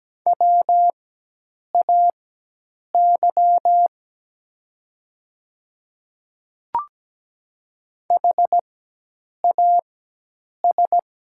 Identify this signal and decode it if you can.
Morse code, heard as a keyed tone, about 17 wpm, spaced as for 8 wpm, 700 Hz. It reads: WAY HAS
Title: 4x Character Spacing - Top 100 Words - 17wpm